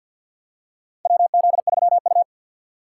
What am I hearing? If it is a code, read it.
Morse code, heard as a keyed tone, about 46 wpm, 700 Hz. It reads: WZ3U
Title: USA Call Signs - 4 Characters - 2x Character Spacing - 50wpm